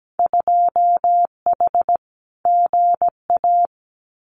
Morse code, heard as a keyed tone, about 17 wpm, 700 Hz. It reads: 2H GA
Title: Winter Field Day Exchanges Encore 17wpm